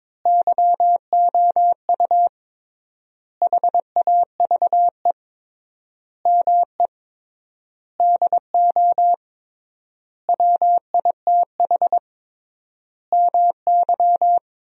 Morse code, heard as a keyed tone, about 22 wpm, 700 Hz. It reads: YOU HAVE ME DO WITH MY
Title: 3x Word-Spacing - Sentences from Top 300 Words - 22wpm